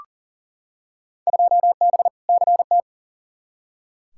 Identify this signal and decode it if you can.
Morse code, heard as a keyed tone, about 40 wpm, 700 Hz. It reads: 2B CT
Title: Field Day Exchanges Encore - 40wpm